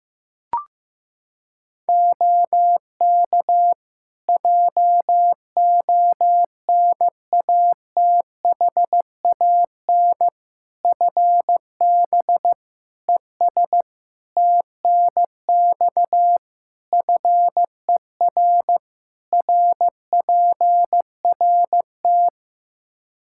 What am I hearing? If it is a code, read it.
Morse code, heard as a keyed tone, about 16 wpm, 700 Hz. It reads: OK JONATHAN FB ES TNX FER RPRT
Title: Practice QSO - Exchange 2 - Info - 15wpm